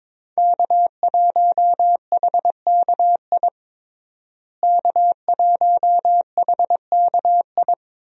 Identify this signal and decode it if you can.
Morse code, heard as a keyed tone, about 22 wpm, 700 Hz. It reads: K1HKI K1HKI
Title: USA Call Signs - 5 Characters - Repeated 2x - 22wpm